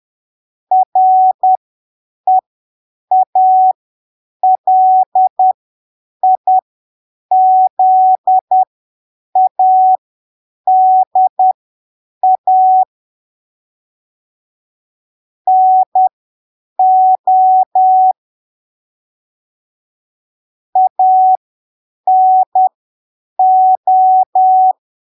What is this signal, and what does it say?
Morse code, heard as a keyed tone, about 10 wpm, spaced as for 7 wpm, 750 Hz. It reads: REALIZADA NO ANO
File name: arquivos / texto / texto 07 10ppm